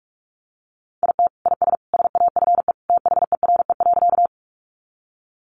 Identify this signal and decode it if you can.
Morse code, heard as a keyed tone, about 46 wpm, 700 Hz. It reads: IT IS SAFE THERE.
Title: Easy Phrases 45wpm